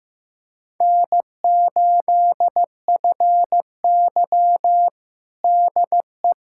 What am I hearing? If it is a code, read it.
Morse code, heard as a keyed tone, about 15 wpm, 700 Hz. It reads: N8FY DE
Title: USA Callsign DE Callsign - 4 Characters - 15wpm